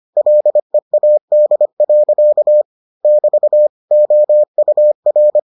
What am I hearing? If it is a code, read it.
Morse code, heard as a keyed tone, about 25 wpm, 600 Hz. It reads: LEAD. <BT> OUR